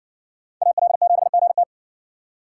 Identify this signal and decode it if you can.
Morse code, heard as a keyed tone, about 54 wpm, 700 Hz. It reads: AL7GT